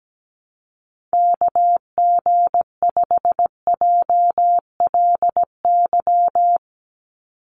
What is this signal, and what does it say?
Morse code, heard as a keyed tone, about 17 wpm, 700 Hz. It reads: KG5JLY
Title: USA Call Signs 17wpm